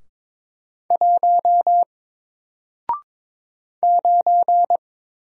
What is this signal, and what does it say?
Morse code, heard as a keyed tone, about 22 wpm, 700 Hz. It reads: 1 9
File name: Single-Letter-Number-22wpm